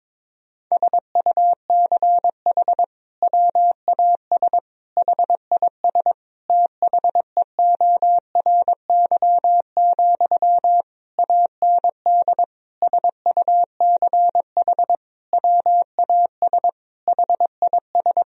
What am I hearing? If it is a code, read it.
Morse code, heard as a keyed tone, about 22 wpm, 700 Hz. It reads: SUCH WAS HIS THEORY, AND SUCH WAS HIS
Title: Sentences from Top 1000 Words - Review Part 4 - 22wpm